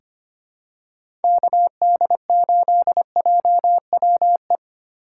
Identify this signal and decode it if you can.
Morse code, heard as a keyed tone, about 25 wpm, 700 Hz. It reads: KD8JWE
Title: USA Call Signs - 6 Characters - 3x Character Spacing - 25wpm